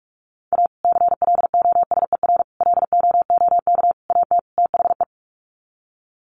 Morse code, heard as a keyed tone, about 44 wpm, 700 Hz. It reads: A CLOSER LOOK AT THE
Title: Sets of 5 Words Encore 45wpm